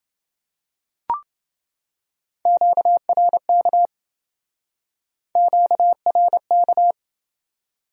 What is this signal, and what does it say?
Morse code, heard as a keyed tone, about 27 wpm, 700 Hz. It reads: QRK QRK